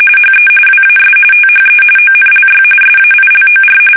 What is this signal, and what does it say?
RTTY, about 45.45 baud. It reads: HE WAS THE ONLY SAILOR
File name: fsk850 / battleshipnj